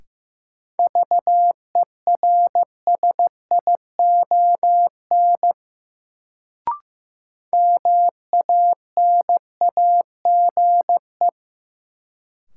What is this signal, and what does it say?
Morse code, heard as a keyed tone, about 16 wpm, 700 Hz. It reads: VERSION MANAGE
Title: Sentences from Top 1400 Words - New Word Drill Alt - 15wpm